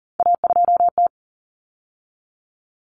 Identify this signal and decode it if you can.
Morse code, heard as a keyed tone, about 40 wpm, 700 Hz. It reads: A2T
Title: Common Cut Numbers - 3-Digit 40wpm